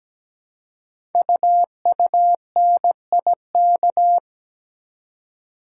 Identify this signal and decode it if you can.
Morse code, heard as a keyed tone, about 17 wpm, 700 Hz. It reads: UUNIK
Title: Code Groups - 5 Letters - 17wpm